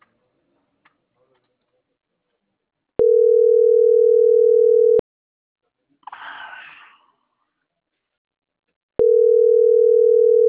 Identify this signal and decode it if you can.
Ringback tone (North America), 440 + 480 Hz.